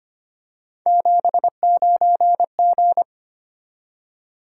Morse code, heard as a keyed tone, about 25 wpm, 700 Hz. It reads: 79G
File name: Three Letter-Number Combinations - 25wpm